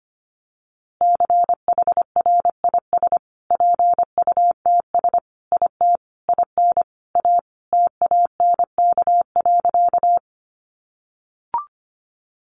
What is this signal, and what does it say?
Morse code, heard as a keyed tone, about 25 wpm, 700 Hz. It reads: CHRIS PUTS IT IN A TANK.
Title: Easy Phrases 25wpm